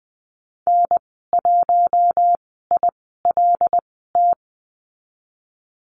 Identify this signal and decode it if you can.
Morse code, heard as a keyed tone, about 20 wpm, spaced as for 14 wpm, 700 Hz. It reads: N1ILT
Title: 076 Lesson 13g - Letter L - Call Signs 20wpm